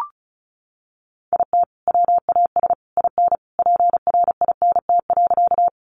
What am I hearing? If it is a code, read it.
Morse code, heard as a keyed tone, about 34 wpm, 700 Hz. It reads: IT WAS IN PRINT.
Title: Sentences from Top 1400 Words 35wpm